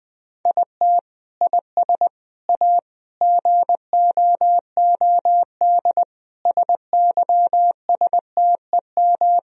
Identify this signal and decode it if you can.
Morse code, heard as a keyed tone, about 20 wpm, 700 Hz. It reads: IT IS A GOOD SYSTEM
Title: Sentences from Top 1000 Words - Review Part 3 - 20wpm